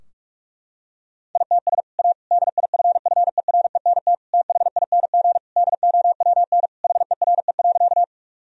Morse code, heard as a keyed tone, about 46 wpm, 700 Hz. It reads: ITS A DIFFERENT THING DOWN HERE.
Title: Sentences from Top 1000 Words - Review Part 3 - 45wpm